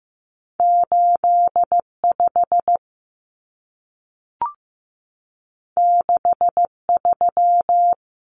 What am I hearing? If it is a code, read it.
Morse code, heard as a keyed tone, about 15 wpm, 700 Hz. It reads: 85 63
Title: Numbers - 2-Digit 15wpm